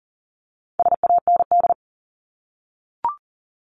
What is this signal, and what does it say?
Morse code, heard as a keyed tone, about 40 wpm, 700 Hz. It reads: SAND
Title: Top 1501-1750 Words 40wpm